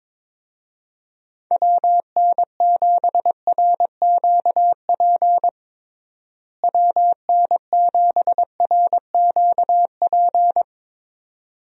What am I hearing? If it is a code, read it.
Morse code, heard as a keyed tone, about 22 wpm, 700 Hz. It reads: WN7RQP WN7RQP
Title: USA Call Signs - 6 Characters - Repeated 2x - 22wpm